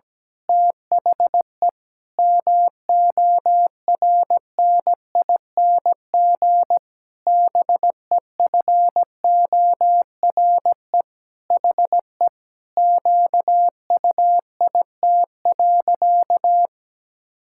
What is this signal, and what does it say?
Morse code, heard as a keyed tone, about 17 wpm, 700 Hz. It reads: THE MORNING BEFORE HE QUIT.
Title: Pangram Challenge v2 - 17wpm